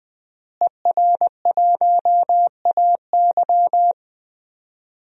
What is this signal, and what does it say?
Morse code, heard as a keyed tone, about 21 wpm, 700 Hz. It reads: ER1AY
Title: DXCC-100 Call Signs - 5 Characters - Repeated 2x - 20wpm